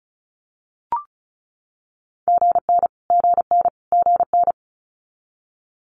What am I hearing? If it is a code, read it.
Morse code, heard as a keyed tone, about 35 wpm, 700 Hz. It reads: GN GN GN
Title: Multi-letter Phonemes Repeated 3x 35wpm